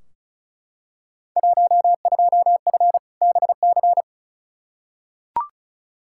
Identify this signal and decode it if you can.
Morse code, heard as a keyed tone, about 36 wpm, 700 Hz. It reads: 12F BC
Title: Field Day Exchanges Encore - 35wpm